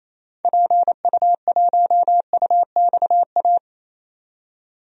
Morse code, heard as a keyed tone, about 28 wpm, 700 Hz. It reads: PU1UXA
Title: DXCC-100 Call Signs - 6 Characters - 28wpm